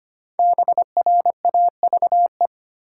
Morse code, heard as a keyed tone, about 25 wpm, 700 Hz. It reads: BRAVE
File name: 5 Letter Words - First 3 Letters Given - 25wpm